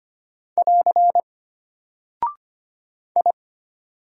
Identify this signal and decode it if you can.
Morse code, heard as a keyed tone, about 25 wpm, 700 Hz. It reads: <AR> I
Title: Single Character (Letter:Number:Punctuation:Prosign) Alt 25wpm